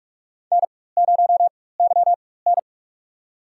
Morse code, heard as a keyed tone, about 44 wpm, spaced as for 21 wpm, 700 Hz. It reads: N0YN
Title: USA Call Signs - 4 Characters - 4x Character Spacing - 45wpm